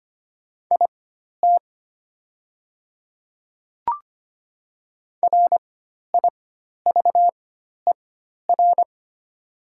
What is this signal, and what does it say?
Morse code, heard as a keyed tone, about 25 wpm, spaced as for 12 wpm, 700 Hz. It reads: IT RIVER
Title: 4x Character Spacing - Top 300 Words - 25wpm